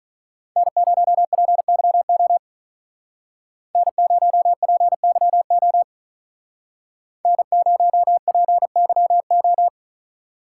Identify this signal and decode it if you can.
Morse code, heard as a keyed tone, about 43 wpm, 700 Hz. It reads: N0PYO N0PYO N0PYO
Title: Speed Racing - USA 1x3 Call Signs 35wpm